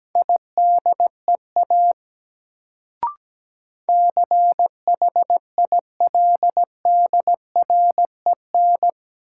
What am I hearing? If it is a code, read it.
Morse code, heard as a keyed tone, about 17 wpm, 700 Hz. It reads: IDEA CHILDREN
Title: Sentences from Top 300 Words - New Word Drill - 17wpm